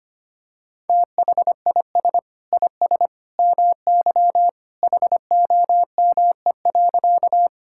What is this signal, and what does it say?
Morse code, heard as a keyed tone, about 25 wpm, 700 Hz. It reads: THIS IS MY HOME.
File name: practice / 25wpm / Sentences with 2-Word Phrases - 25wpm